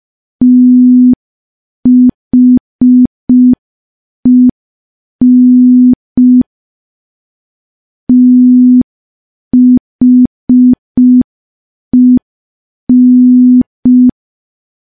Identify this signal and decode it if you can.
Morse code, heard as a keyed tone, about 5 wpm, 250 Hz. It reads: THEN THEN
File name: morse / morse-training-05WPM